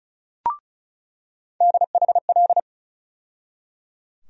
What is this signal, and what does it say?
Morse code, heard as a keyed tone, about 36 wpm, 700 Hz. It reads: DHL
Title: USA Call Sign Postfixes - 3 Letters 35wpm